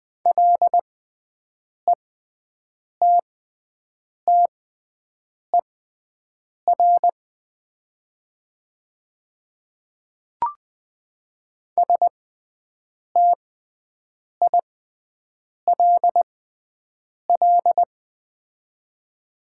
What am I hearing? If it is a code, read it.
Morse code, heard as a keyed tone, about 20 wpm, spaced as for 7 wpm, 700 Hz. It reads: LETTER STILL
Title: 6x Character Spacing - Top 300 Words - 20wpm